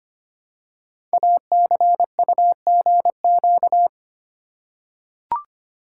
Morse code, heard as a keyed tone, about 25 wpm, 700 Hz. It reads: ACUGQ
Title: Code Groups - 5 Letters - 25wpm